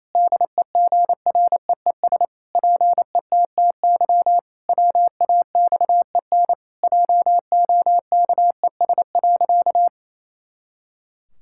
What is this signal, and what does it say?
Morse code, heard as a keyed tone, about 28 wpm, 700 Hz. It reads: DEGREES PETTY WAXEN JOKES.
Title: Pangram Challenge v2 - 28wpm